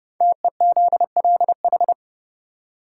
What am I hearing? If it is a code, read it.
Morse code, heard as a keyed tone, about 30 wpm, 700 Hz. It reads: TEZLH
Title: Code Groups - 5 Letters - 30wpm